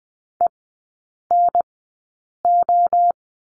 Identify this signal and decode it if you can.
Morse code, heard as a keyed tone, about 20 wpm, spaced as for 8 wpm, 700 Hz. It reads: ENO
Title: 081 Lesson 14e - Number 2 - Three Characters 20wpm